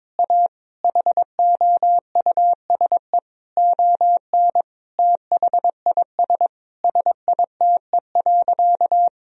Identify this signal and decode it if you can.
Morse code, heard as a keyed tone, about 22 wpm, 700 Hz. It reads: A HOUSE ON THIS SITE.
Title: Sentences from Top 1000 Words - Review Part 4 - 22wpm